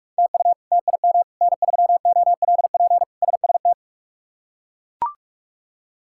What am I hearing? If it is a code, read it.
Morse code, heard as a keyed tone, about 41 wpm, 700 Hz. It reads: TU TIM N3OLP SST
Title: K1USN SST QSO Practice - 45wpm